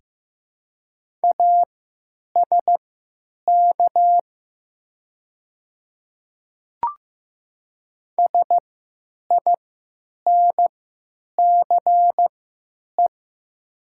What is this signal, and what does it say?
Morse code, heard as a keyed tone, about 15 wpm, spaced as for 8 wpm, 700 Hz. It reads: ASK SINCE
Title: 3x Character Spacing - Top 500 Words - 15wpm